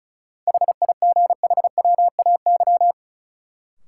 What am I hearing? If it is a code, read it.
Morse code, heard as a keyed tone, about 36 wpm, 700 Hz. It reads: HIGHWAY
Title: Sentences from Top 1400 Words - New Word Drill - 35wpm